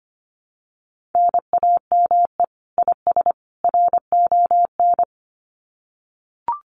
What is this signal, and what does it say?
Morse code, heard as a keyed tone, about 25 wpm, 700 Hz. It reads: NAME IS RON
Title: Common QSO Phrases 25wpm